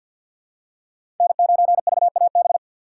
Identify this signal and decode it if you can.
Morse code, heard as a keyed tone, about 47 wpm, 700 Hz. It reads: N9VAB
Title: USA Call Signs - 5 Characters - 3x Character Spacing - 50wpm